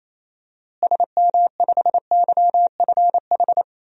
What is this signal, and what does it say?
Morse code, heard as a keyed tone, about 28 wpm, 700 Hz. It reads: SM5YFH
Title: DXCC-100 Call Signs - 6 Characters - Repeated 2x - 28wpm